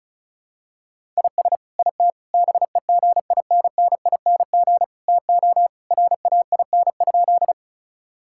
Morse code, heard as a keyed tone, about 36 wpm, 700 Hz. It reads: IS IT BEGINNING TO RAIN?
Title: Sentences from Top 1000 Words - Review Part 5 - 35wpm